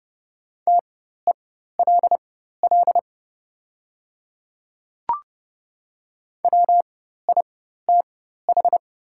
Morse code, heard as a keyed tone, about 31 wpm, spaced as for 14 wpm, 700 Hz. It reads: TELL WITH
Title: 4x Character Spacing - Top 200 Words - 30wpm